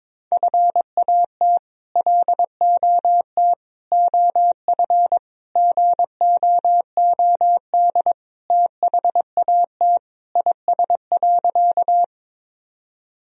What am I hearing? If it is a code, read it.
Morse code, heard as a keyed tone, about 22 wpm, 700 Hz. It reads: FAT LOT OF GOOD THAT IS.
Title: Sentences from Top 1000 Words - Review Part 4 - 22wpm